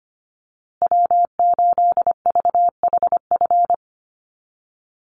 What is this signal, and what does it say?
Morse code, heard as a keyed tone, about 25 wpm, 700 Hz. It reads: W8VHF
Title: USA Call Signs Repeated 2x 25wpm